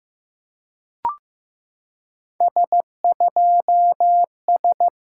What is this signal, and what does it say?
Morse code, heard as a keyed tone, about 14 wpm, 690 Hz. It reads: S2S
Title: SOTA QSO Practice 15wpm